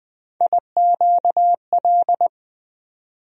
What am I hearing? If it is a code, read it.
Morse code, heard as a keyed tone, about 20 wpm, 700 Hz. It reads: IQL
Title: USA Call Sign Postfixes - 3 Letters 20wpm